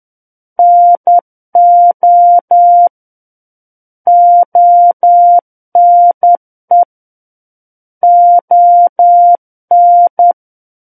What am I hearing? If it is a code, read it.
Morse code, heard as a keyed tone, about 10 wpm, 700 Hz. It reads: NO ONE ON
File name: Book One / 01-Chapter 1 - 10 WPM